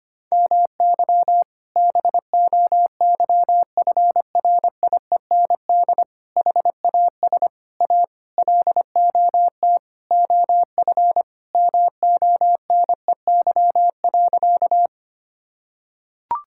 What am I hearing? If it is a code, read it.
Morse code, heard as a keyed tone, about 25 wpm, 700 Hz. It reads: MY BOYFRIEND HAS A LOT OF MONEY.
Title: Sentences from Top 1000 Words - Review Part 5 - 25wpm